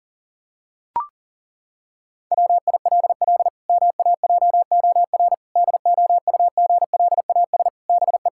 Morse code, heard as a keyed tone, about 40 wpm, 700 Hz. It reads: WILL MAJOR DOUGLAS BE
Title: Pangram Challenge v2 - 40wpm